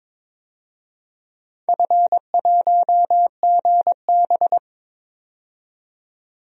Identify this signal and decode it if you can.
Morse code, heard as a keyed tone, about 21 wpm, 700 Hz. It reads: F1GB
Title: DXCC-100 Call Signs - 4 Characters - 22wpm